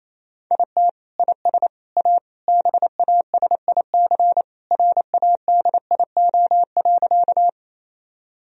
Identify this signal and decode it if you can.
Morse code, heard as a keyed tone, about 27 wpm, 700 Hz. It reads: IT IS A BASIC RADIO.
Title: Sentences from Top 1000 Words - Review Part 3 - 28wpm